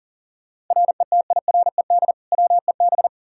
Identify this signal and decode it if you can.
Morse code, heard as a keyed tone, about 38 wpm, 700 Hz. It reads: RETIRED WEB
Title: Practice QSO - Exchange 3 - More Info - 40wpm